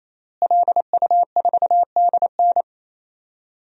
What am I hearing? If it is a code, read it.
Morse code, heard as a keyed tone, about 28 wpm, 700 Hz. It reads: LU4DN
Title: DXCC-100 Call Signs - 5 Characters - Repeated 3x - 28wpm